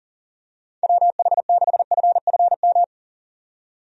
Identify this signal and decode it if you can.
Morse code, heard as a keyed tone, about 40 wpm, 700 Hz. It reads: WH6FFM
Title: USA Call Signs - 6 Characters - 40wpm